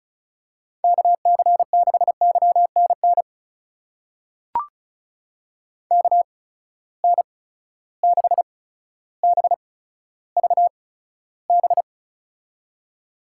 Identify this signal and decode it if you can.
Morse code, heard as a keyed tone, about 36 wpm, spaced as for 15 wpm, 700 Hz. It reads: KC6YNN KN6BVB
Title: USA Call Signs - 6 Characters - 8x Character Spacing - 35wpm